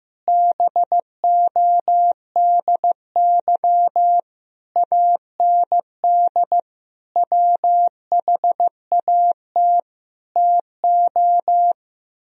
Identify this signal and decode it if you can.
Morse code, heard as a keyed tone, about 15 wpm, 700 Hz. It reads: BODY AND WHAT TO
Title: Sentences from Top 1000 Words - Review Part 2 - 15wpm